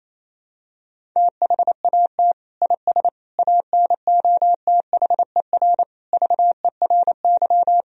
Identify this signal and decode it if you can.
Morse code, heard as a keyed tone, about 28 wpm, 700 Hz. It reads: THAT IS ANOTHER VERY
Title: Sentences from Top 1000 Words - Review Part 2 - 28wpm